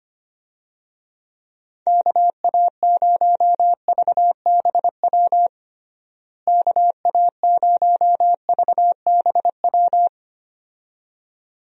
Morse code, heard as a keyed tone, about 25 wpm, 700 Hz. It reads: KA0VBW KA0VBW
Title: USA Call Signs - 6 Characters - Repeated 2x - 25wpm